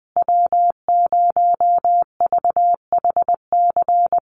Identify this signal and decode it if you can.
Morse code, heard as a keyed tone, about 20 wpm, 700 Hz. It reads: W0VHC
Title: USA Call Signs Repeated 3x 20wpm